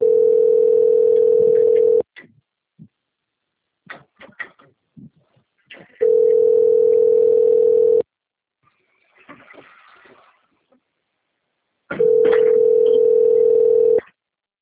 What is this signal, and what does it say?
Ringback tone (North America), 440 + 480 Hz.
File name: out- - - - - .19530